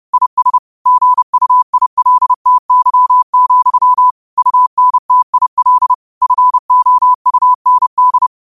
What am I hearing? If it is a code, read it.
Morse code, heard as a keyed tone, about 30 wpm, 1000 Hz. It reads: IS GUILTY, UNTIL FOUND
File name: qotdfiles / 30wpm